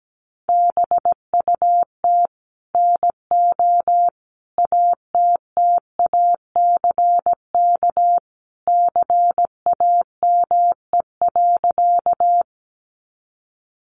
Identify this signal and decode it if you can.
Morse code, heard as a keyed tone, about 17 wpm, 700 Hz. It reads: BUT NO ATTACK CAME.
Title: Sentences from Top 1500 Words 17wpm